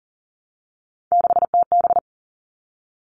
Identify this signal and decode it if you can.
Morse code, heard as a keyed tone, about 40 wpm, 700 Hz. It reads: 6TB
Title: All Cut Numbers - 3-Digit 40wpm